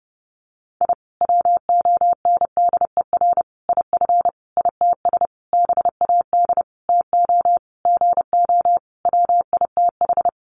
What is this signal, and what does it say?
Morse code, heard as a keyed tone, about 30 wpm, 700 Hz. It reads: I WONDER IF ITS BAD TO GO WITH
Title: Sentences from Top 700 Words 30wpm